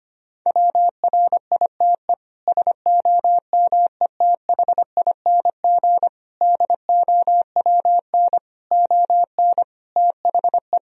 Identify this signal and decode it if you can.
Morse code, heard as a keyed tone, about 25 wpm, 700 Hz. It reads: WRITE SOMETHING DOWN ON THE
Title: Sentences from Top 1000 Words - Review Part 1 - 25wpm